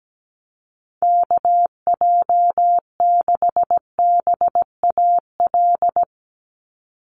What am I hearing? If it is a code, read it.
Morse code, heard as a keyed tone, about 17 wpm, 700 Hz. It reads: KJ6BAL
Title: USA Call Signs 17wpm